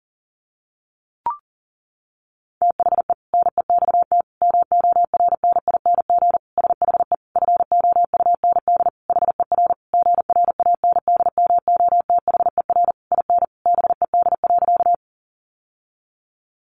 Morse code, heard as a keyed tone, about 40 wpm, 700 Hz. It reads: THE NEXT MORNING SHE FOUND HER GRANDMOTHER IN BED.
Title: Sentences from Top 1200 Words 40wpm